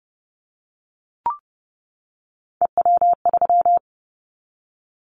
Morse code, heard as a keyed tone, about 30 wpm, 700 Hz. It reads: EW3
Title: Callsign Prefixes for Top 50 DXCC - 30wpm